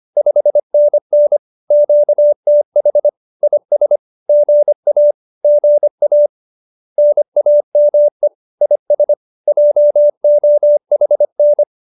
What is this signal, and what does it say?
Morse code, heard as a keyed tone, about 25 wpm, 590 Hz. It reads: 5NN QTH IS GA GA NAME IS JOHN